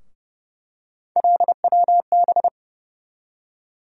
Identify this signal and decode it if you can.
Morse code, heard as a keyed tone, about 30 wpm, 700 Hz. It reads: LWB
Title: USA Call Sign Postfixes - 3 Letters 30wpm